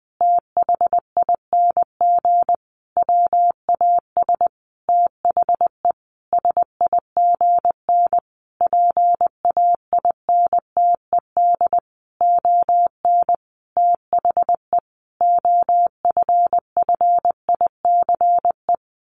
Morse code, heard as a keyed tone, about 20 wpm, 700 Hz. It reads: THING WAS THE SIGN PAINTED ON THE OFFICE
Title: A Sound Of Thunder 20wpm